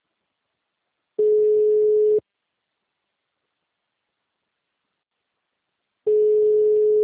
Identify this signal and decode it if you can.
Ringback tone (Europe, 425 Hz).
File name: out- -2029